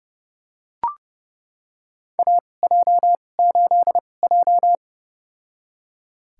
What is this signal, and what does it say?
Morse code, heard as a keyed tone, about 32 wpm, spaced as for 22 wpm, 700 Hz. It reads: AJ8J
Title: USA Call Signs - 4 Characters - 2x Character Spacing - 30wpm